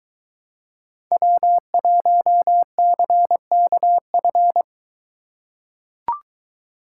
Morse code, heard as a keyed tone, about 23 wpm, 700 Hz. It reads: W1CKF